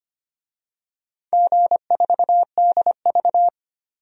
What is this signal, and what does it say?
Morse code, heard as a keyed tone, about 25 wpm, 700 Hz. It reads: G4DV
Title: DXCC-100 Call Signs - 4 Characters - 25wpm